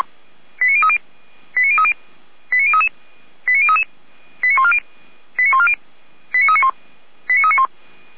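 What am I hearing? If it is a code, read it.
Five-tone selective call (ZVEI), heard as quick runs of five tones, tones of about 70 ms. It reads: 89030 89030 89033 89033 89140 89140 89391 89391
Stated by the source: ZVEI five-tone selective call received with an SDR.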